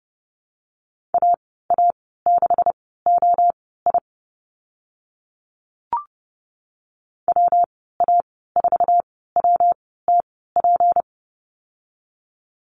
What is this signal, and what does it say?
Morse code, heard as a keyed tone, about 30 wpm, spaced as for 17 wpm, 700 Hz. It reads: AA6OI WA4WTP
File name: Lesson 37g - Letter x - Call Signs 30wpm